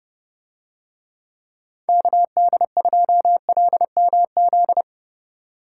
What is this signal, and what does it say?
Morse code, heard as a keyed tone, about 29 wpm, 700 Hz. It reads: KD2LMZ